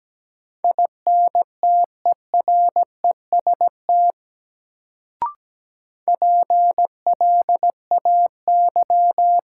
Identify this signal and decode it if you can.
Morse code, heard as a keyed tone, about 17 wpm, 700 Hz. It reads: INTEREST PLAY
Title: Sets of 5 Words Encore - New Word Drill - 17wpm